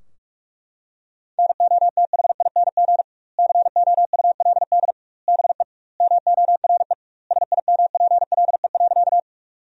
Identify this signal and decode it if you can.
Morse code, heard as a keyed tone, about 42 wpm, 700 Hz. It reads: NOTHING COULD BE MORE SIMPLE.
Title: 2x Word-Spacing - Sentences from Top 500 Words - 45wpm